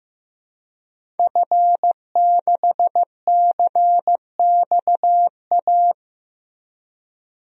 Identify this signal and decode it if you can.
Morse code, heard as a keyed tone, about 15 wpm, 700 Hz. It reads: F6CXA
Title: DXCC-100 Call Signs - 5 Characters - Repeated 3x - 15wpm